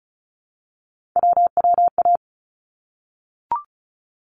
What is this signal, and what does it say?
Morse code, heard as a keyed tone, about 36 wpm, 700 Hz. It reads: WWA
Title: ARRL Sections 35wpm